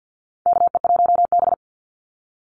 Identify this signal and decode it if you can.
Morse code, heard as a keyed tone, about 50 wpm, 700 Hz. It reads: KE1B
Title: Call Signs Repeated 2x - 50wpm